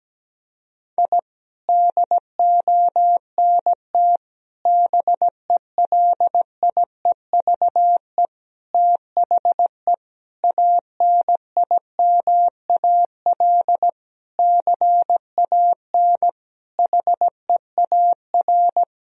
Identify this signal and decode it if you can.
Morse code, heard as a keyed tone, about 17 wpm, 700 Hz. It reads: I DONT BELIEVE THE ANIMAL CAN HEAR
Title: Sentences from Top 1000 Words - Review Part 1 - 17wpm